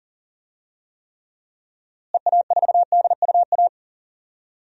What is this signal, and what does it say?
Morse code, heard as a keyed tone, about 37 wpm, 700 Hz. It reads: EA4DUA
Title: DXCC-100 Call Signs - 6 Characters - Repeated 3x - 40wpm